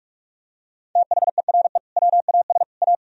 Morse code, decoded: THERE WAS A

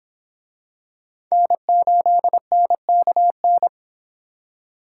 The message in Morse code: N8NKN

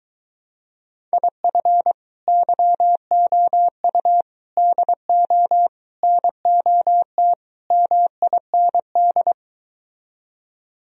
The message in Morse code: IF YOU DO NOT MIND

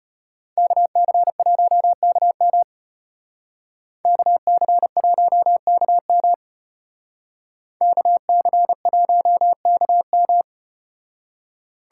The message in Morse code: KC1KM KC1KM KC1KM

